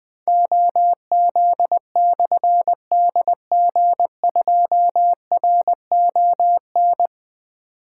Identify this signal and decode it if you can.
Morse code, heard as a keyed tone, about 20 wpm, 700 Hz. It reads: OZ/DG2RON
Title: Difficult Call Signs 20wpm